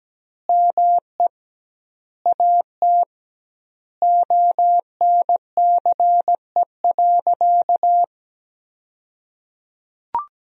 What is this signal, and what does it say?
Morse code, heard as a keyed tone, about 17 wpm, 700 Hz. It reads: ME AT ONCE.
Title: 2x Word-Spacing - Sentences from Top 500 Words - 17wpm